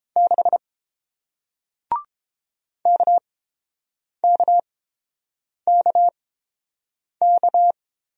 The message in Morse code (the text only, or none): 6 KKKK